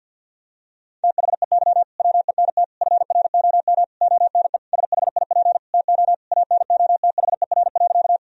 Morse code, heard as a keyed tone, about 53 wpm, 700 Hz. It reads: THEY WENT FROM ONE SHIP TO ANOTHER.
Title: Sentences from Top 1000 Words - Review Part 2 - 50wpm